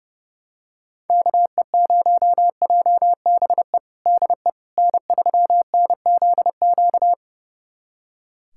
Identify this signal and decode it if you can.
Morse code, heard as a keyed tone, about 32 wpm, 700 Hz. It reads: KE0JBE DE N3NZQ
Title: USA Callsign DE Callsign 30wpm